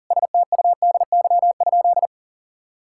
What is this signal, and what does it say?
Morse code, heard as a keyed tone, about 40 wpm, 700 Hz. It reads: STUDY?
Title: Sentences from Top 1000 Words - Review Part 6 - 40wpm